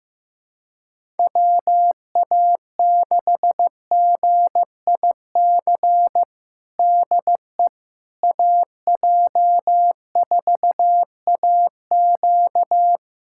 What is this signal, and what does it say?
Morse code, heard as a keyed tone, about 15 wpm, 700 Hz. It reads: WA6GIC DE AJ4AQ
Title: USA Callsign DE Callsign 15wpm